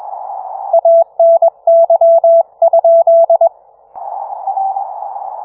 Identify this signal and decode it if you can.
Morse code, heard as a keyed tone, about 22 wpm, 670 Hz. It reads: ANY?